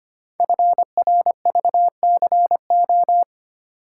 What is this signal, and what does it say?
Morse code, heard as a keyed tone, about 25 wpm, 700 Hz. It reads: FRVCO
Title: Code Groups - 5 Letters - 25wpm